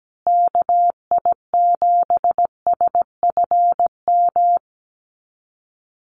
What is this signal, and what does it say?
Morse code, heard as a keyed tone, about 17 wpm, 700 Hz. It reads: KI7SFM
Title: USA Call Signs Repeated 3x 17wpm